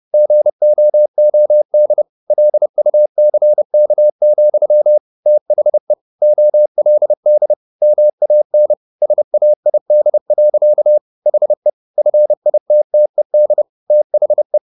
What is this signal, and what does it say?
Morse code, heard as a keyed tone, about 30 wpm, 600 Hz. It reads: GOOD LUCK, THE OLD MAN SAID. HE FITTED THE
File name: the old man and the sea / the old man and the sea 1 30wpm